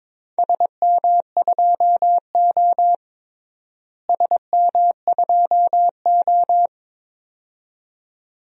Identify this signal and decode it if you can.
Morse code, heard as a keyed tone, about 21 wpm, 700 Hz. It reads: SM2O SM2O
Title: DXCC-100 Call Signs - 4 Characters - Repeated 2x - 22wpm